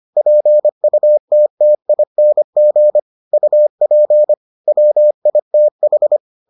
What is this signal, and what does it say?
Morse code, heard as a keyed tone, about 25 wpm, 600 Hz. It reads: PUTTING UP WITH